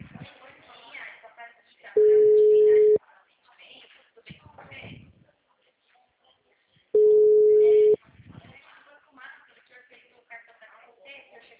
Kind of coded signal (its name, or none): telephone call progress tone